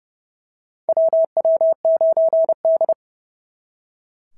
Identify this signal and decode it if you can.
Morse code, heard as a keyed tone, about 30 wpm, 650 Hz. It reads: WW9D
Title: SOTA QSO Practice 30wpm